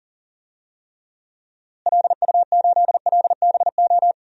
Morse code, heard as a keyed tone, about 40 wpm, 700 Hz. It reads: LU8LBO
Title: DXCC-100 Call Signs - 6 Characters - Repeated 2x - 40wpm